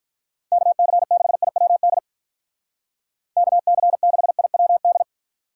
Morse code, heard as a keyed tone, about 45 wpm, 700 Hz. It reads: KC6IWD KC6IWD